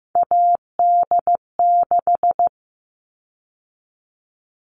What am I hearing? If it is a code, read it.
Morse code, heard as a keyed tone, about 15 wpm, 700 Hz. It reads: AD6
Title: All Cut Numbers - 3-Digit 15wpm